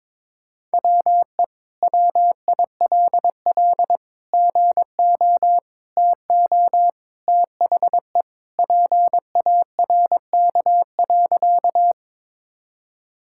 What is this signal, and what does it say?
Morse code, heard as a keyed tone, about 22 wpm, 700 Hz. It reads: WE WILL GO TO THE PARK.
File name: practice / 22wpm / Sentences with 2-Word Phrases - 22wpm